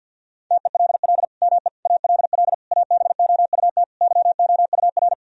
Morse code, heard as a keyed tone, about 50 wpm, 700 Hz. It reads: TELL ME ALL ABOUT YOUR
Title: Sentences from Top 1000 Words - Review Part 4 - 50wpm